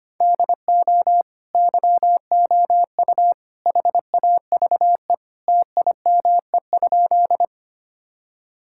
Morse code, heard as a keyed tone, about 25 wpm, 700 Hz. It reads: DO YOU HAVE TIME?
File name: practice / 25wpm / Sentences with 3-Word Phrases - 25wpm